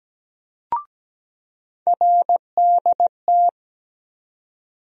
Morse code, heard as a keyed tone, about 17 wpm, 700 Hz. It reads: RDT
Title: USA Call Sign Postfixes - 3 Letters 17wpm